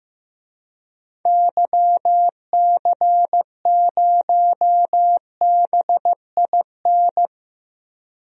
Morse code, heard as a keyed tone, about 15 wpm, 700 Hz. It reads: YC0BIN
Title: DXCC-100 Call Signs - 6 Characters - Repeated 3x - 15wpm